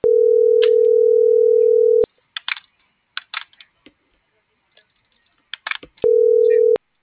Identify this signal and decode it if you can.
Ringback tone (North America), 440 + 480 Hz.